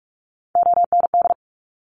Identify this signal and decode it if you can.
Morse code, heard as a keyed tone, about 46 wpm, 700 Hz. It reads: OND
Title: Top Three Letter Combinations in Words 45wpm